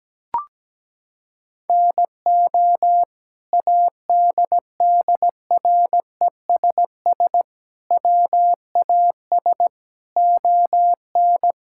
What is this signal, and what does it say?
Morse code, heard as a keyed tone, about 17 wpm, 700 Hz. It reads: NO ADDRESS WAS ON
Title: Sentences from Top 1000 Words - Review Part 6 - 17wpm